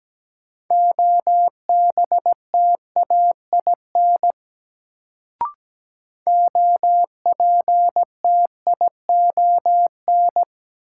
Morse code, heard as a keyed tone, about 17 wpm, 700 Hz. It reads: OBTAIN OPTION
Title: Sentences from Top 1300 Words - New Word Drill Alt - 17wpm